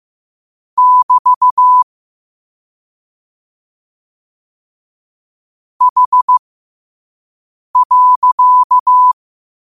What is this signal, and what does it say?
Morse code, heard as a keyed tone, about 15 wpm, spaced as for 5 wpm, 1000 Hz. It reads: <BT> H.
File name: qotdfiles / 5wpm